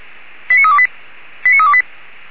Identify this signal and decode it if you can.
Five-tone selective call (ZVEI), heard as quick runs of five tones, tones of about 70 ms. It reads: 89328 89328